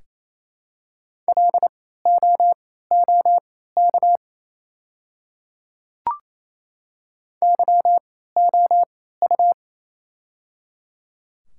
Morse code, heard as a keyed tone, about 27 wpm, spaced as for 16 wpm, 700 Hz. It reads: LOOK YOU
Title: 3x Character Spacing - Top 200 Words - 28wpm